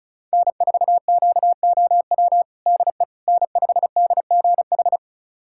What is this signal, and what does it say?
Morse code, heard as a keyed tone, about 34 wpm, 700 Hz. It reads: N4QOW DE N5DGH
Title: USA Callsign DE Callsign - 5 Characters - 35wpm